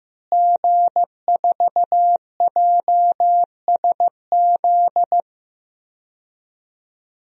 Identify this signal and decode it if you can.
Morse code, heard as a keyed tone, about 15 wpm, 700 Hz. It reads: G4JSZ
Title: DXCC-100 Call Signs - 5 Characters - Repeated 3x - 15wpm